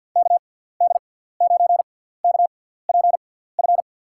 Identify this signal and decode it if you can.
Morse code, heard as a keyed tone, about 51 wpm, spaced as for 17 wpm, 700 Hz. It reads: KD9KPF
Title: USA Call Signs - 6 Characters - 6x Character Spacing - 50wpm